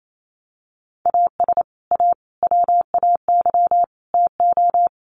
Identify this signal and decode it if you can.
Morse code, heard as a keyed tone, about 28 wpm, 700 Hz. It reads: AS A WAY TO
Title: Sets of 4 Words 28wpm